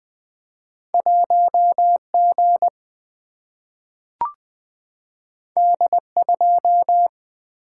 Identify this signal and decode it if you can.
Morse code, heard as a keyed tone, about 20 wpm, 700 Hz. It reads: 1 G D 2
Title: Single-Letter Single-Number 20wpm